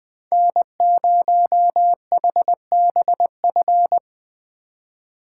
Morse code, heard as a keyed tone, about 20 wpm, 700 Hz. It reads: N0HBF
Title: Speed Racing - USA 1x3 Call Signs 17wpm